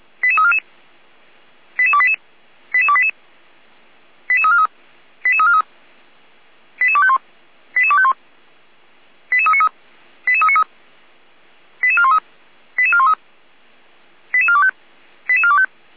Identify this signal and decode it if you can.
Five-tone selective call (ZVEI), heard as quick runs of five tones, tones of about 70 ms. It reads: 88344 88288 88288 88343 88343 88251 88251 88393 88393 88413 88413 88425 88425